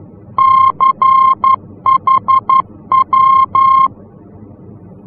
Morse code, heard as a keyed tone, about 11 wpm, 1060 Hz. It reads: CHW
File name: france / mx / CHW-F-VOR